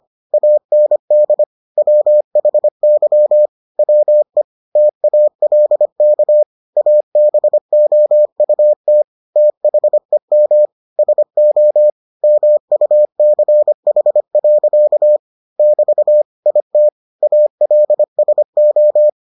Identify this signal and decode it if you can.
Morse code, heard as a keyed tone, about 25 wpm, 600 Hz. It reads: AND WHY WE TALK ABOUT THEM SO MUCH. <BT> IT ALSO